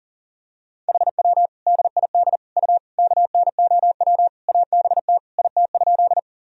Morse code, heard as a keyed tone, about 39 wpm, 700 Hz. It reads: HW DID U KNOW ABT IT?
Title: Sentences from Top 100 Words and Abbreviations 40wpm